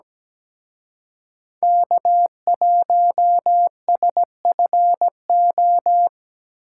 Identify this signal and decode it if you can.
Morse code, heard as a keyed tone, about 20 wpm, 700 Hz. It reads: K1SFO
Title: Speed Racing - USA Call Signs 17wpm